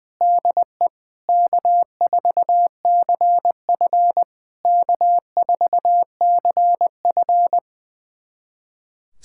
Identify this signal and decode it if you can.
Morse code, heard as a keyed tone, about 20 wpm, 700 Hz. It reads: DE K4CF K4CF
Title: SOTA QSO Practice 20wpm